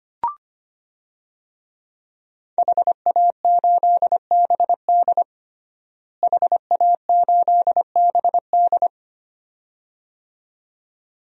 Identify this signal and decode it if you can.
Morse code, heard as a keyed tone, about 25 wpm, 700 Hz. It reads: HA8BD HA8BD
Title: DXCC-100 Call Signs - 5 Characters - Repeated 2x - 25wpm